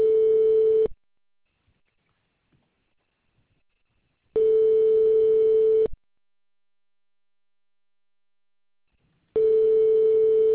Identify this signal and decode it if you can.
Ringback tone (France), 440 Hz.